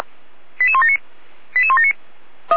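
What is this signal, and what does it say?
Five-tone selective call (ZVEI), heard as quick runs of five tones, tones of about 70 ms. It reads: 88179 88179
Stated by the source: Received by SDR; ZVEI five-tone selective call.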